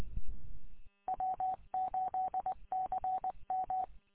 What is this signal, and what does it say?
Morse code, heard as a keyed tone, about 20 wpm, 750 Hz. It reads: W8CM
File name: w8cmn- -combined